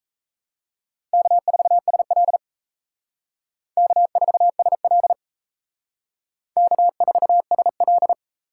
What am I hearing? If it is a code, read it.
Morse code, heard as a keyed tone, about 37 wpm, 700 Hz. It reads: K4SL K4SL K4SL